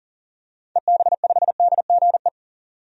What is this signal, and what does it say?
Morse code, heard as a keyed tone, about 40 wpm, 700 Hz. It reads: EB5DGE
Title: DXCC-100 Call Signs - 6 Characters - 40wpm